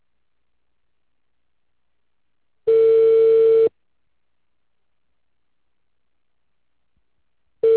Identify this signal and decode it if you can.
Ringback tone (China, 450 Hz).